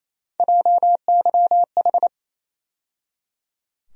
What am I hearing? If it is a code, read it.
Morse code, heard as a keyed tone, about 28 wpm, 700 Hz. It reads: JYH
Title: USA Call Sign Postfixes - 3 Letters 28wpm